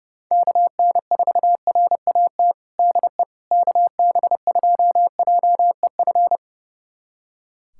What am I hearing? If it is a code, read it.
Morse code, heard as a keyed tone, about 32 wpm, 700 Hz. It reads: KN4RAT DE KB2JEF
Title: USA Callsign DE Callsign - 6 Characters - 30wpm